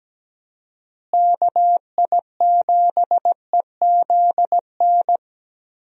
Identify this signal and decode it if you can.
Morse code, heard as a keyed tone, about 17 wpm, 700 Hz. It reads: KI7EZN
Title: USA Call Signs - 6 Characters - Repeated 2x - 17wpm